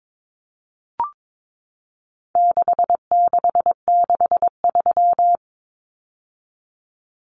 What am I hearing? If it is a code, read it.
Morse code, heard as a keyed tone, about 22 wpm, 700 Hz. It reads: 6663
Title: Numbers - 4-Digit 22wpm